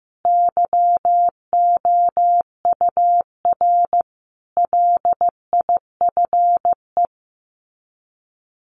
Morse code, heard as a keyed tone, about 15 wpm, 700 Hz. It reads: YOUR LIFE
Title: Sets of 5 Words 15wpm